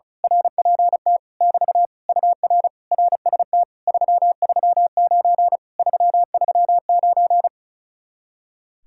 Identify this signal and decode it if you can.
Morse code, heard as a keyed tone, about 33 wpm, 700 Hz. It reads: RPT <BT> UR RST 339 339
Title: Speed-Racing Common QSO Phrases - 35wpm